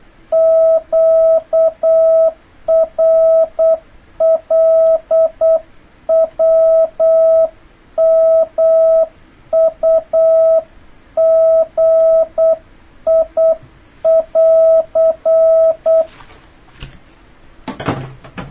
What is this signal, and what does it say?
Morse code, heard as a keyed tone, about 8 wpm, 650 Hz. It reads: QRLWMUGI <AR>
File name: jh8cbh / 2026cw